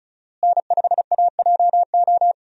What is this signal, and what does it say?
Morse code, heard as a keyed tone, about 36 wpm, 700 Hz. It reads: N5AJO
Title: USA Call Signs - 5 Characters - Repeated 3x - 35wpm